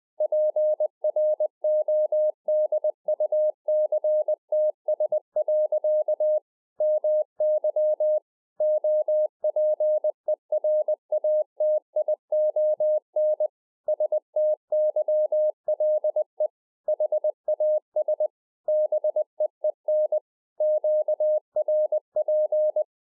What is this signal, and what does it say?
Morse code, heard as a keyed tone, about 20 wpm, 610 Hz. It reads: PRODUCTS. MY OPERATION STYLE HAS BEEN QRP